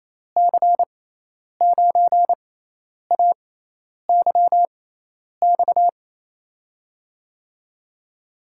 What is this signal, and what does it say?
Morse code, heard as a keyed tone, about 28 wpm, spaced as for 10 wpm, 700 Hz. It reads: C9AYX
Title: USA Call Signs - 6 Characters - 6x Character Spacing - 28wpm